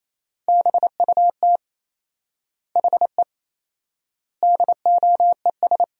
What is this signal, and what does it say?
Morse code, heard as a keyed tone, about 28 wpm, 700 Hz. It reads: BUT HE DOES
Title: 4x Word-Spacing - Sentences from Top 200 Words - 28wpm